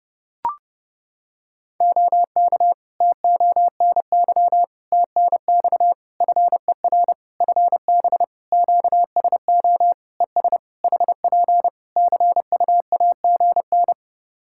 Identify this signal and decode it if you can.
Morse code, heard as a keyed tone, about 30 wpm, 700 Hz. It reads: OK TONY TNX FER FB QSO ES HP CUAGN
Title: Practice QSO - Exchange 4 - Ending - 30wpm